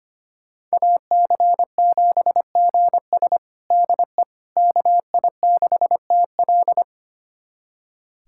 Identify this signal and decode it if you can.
Morse code, heard as a keyed tone, about 25 wpm, 700 Hz. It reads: AC7GS DE KI6TL
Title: USA Callsign DE Callsign - 5 Characters - 25wpm